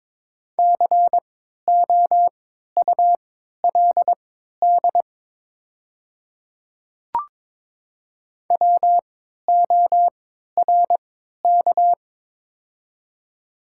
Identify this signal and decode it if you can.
Morse code, heard as a keyed tone, about 22 wpm, spaced as for 12 wpm, 700 Hz. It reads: COULD WORK
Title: 3x Character Spacing - Top 300 Words - 22wpm